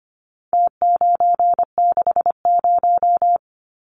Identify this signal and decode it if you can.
Morse code, heard as a keyed tone, about 25 wpm, 700 Hz. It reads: T960
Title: Numbers - Contest Style - 4-Digit 25wpm